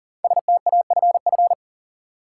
Morse code, heard as a keyed tone, about 40 wpm, 700 Hz. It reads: STAFF
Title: Sets of 5 Words Encore - New Word Drill - 40wpm